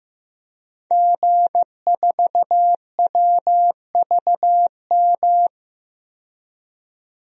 Morse code, heard as a keyed tone, about 15 wpm, 700 Hz. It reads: G4WVM
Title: DXCC-100 Call Signs - 5 Characters - 15wpm